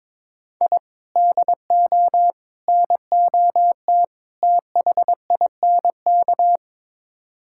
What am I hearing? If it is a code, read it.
Morse code, heard as a keyed tone, about 22 wpm, 700 Hz. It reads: I DO NOT THINK
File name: practice / 22wpm / Build A Phrase - 5-Word Phrases - 22wpm